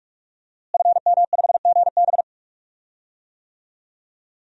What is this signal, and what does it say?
Morse code, heard as a keyed tone, about 46 wpm, 700 Hz. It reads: FM5GB
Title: DXCC-100 Call Signs - 5 Characters - Repeated 3x - 45wpm